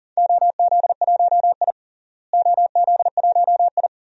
Morse code, heard as a keyed tone, about 40 wpm, 700 Hz. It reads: OZ1I OZ1I
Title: DXCC-100 Call Signs - 4 Characters - Repeated 2x - 40wpm